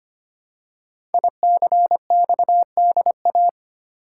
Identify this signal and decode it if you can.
Morse code, heard as a keyed tone, about 25 wpm, 700 Hz. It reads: ICXDA
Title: Code Groups - 5 Letter:Numbers - 25wpm